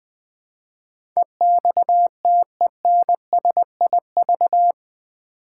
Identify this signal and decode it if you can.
Morse code, heard as a keyed tone, about 20 wpm, 700 Hz. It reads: EXTENSIV
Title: Build A Word - 4 Letters At A Time - 11-Letter Words - 20wpm